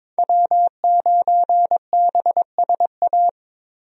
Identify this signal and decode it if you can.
Morse code, heard as a keyed tone, about 22 wpm, 700 Hz. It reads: W9BSA
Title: USA Call Signs - 5 Characters - Repeated 2x - 22wpm